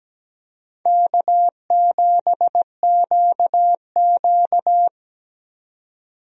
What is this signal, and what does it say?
Morse code, heard as a keyed tone, about 17 wpm, 700 Hz. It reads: K7QQ
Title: USA Call Signs - 4 Characters - 17wpm